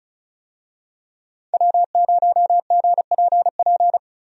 Morse code, heard as a keyed tone, about 36 wpm, 700 Hz. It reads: W0GPP
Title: USA Call Signs - 5 Characters - 35wpm